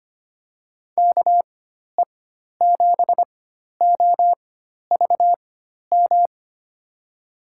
Morse code, heard as a keyed tone, about 25 wpm, spaced as for 12 wpm, 700 Hz. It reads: KE7OVM
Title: USA Call Signs - 6 Characters - 4x Character Spacing - 25wpm